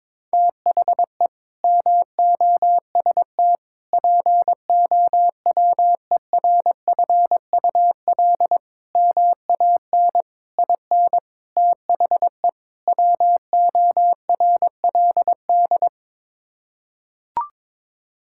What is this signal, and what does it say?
Morse code, heard as a keyed tone, about 22 wpm, 700 Hz. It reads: THE MOST POWERFUL MAN IN THE WORLD
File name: practice / 22wpm / Sets of 7 Words - 22wpm